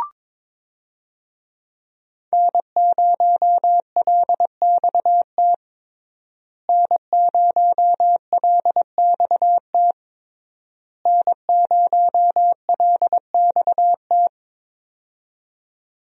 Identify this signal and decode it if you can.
Morse code, heard as a keyed tone, about 21 wpm, 700 Hz. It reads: N0LXT N0LXT N0LXT
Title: USA Call Signs - 5 Characters - Repeated 3x - 22wpm